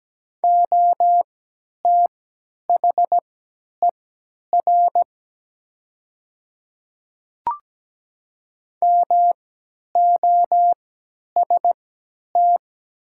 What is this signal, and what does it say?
Morse code, heard as a keyed tone, about 17 wpm, spaced as for 10 wpm, 700 Hz. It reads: OTHER MOST
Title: 3x Character Spacing - Top 100 Words - 17wpm